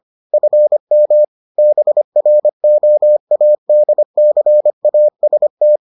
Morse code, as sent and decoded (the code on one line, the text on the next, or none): ..-. -- / -... .-. --- .- -.. -.-. .- ... -
FM BROADCAST